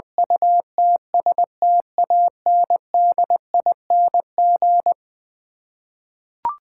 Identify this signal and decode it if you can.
Morse code, heard as a keyed tone, about 20 wpm, 700 Hz. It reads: UTSTANDING
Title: Eleven Letter Words - 20wpm